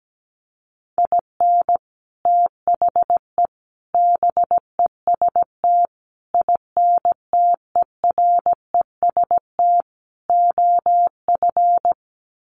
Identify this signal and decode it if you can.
Morse code, heard as a keyed tone, about 17 wpm, 700 Hz. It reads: IN THE BEST INTEREST OF
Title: Sets of 5 Words Encore 17wpm